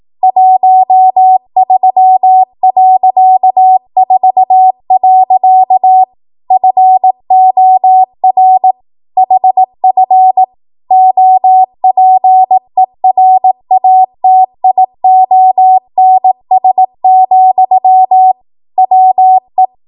Morse code, heard as a keyed tone, about 18 wpm, 750 Hz. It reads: 13.4. FOR HF OPERATIONS, WE
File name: Archive / 18 WPM / 18WPM